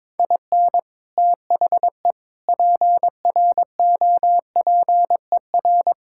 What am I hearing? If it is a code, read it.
Morse code, heard as a keyed tone, about 22 wpm, 700 Hz. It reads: IN THE PROPER
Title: Sentences from Top 1000 Words - Review Part 5 - 22wpm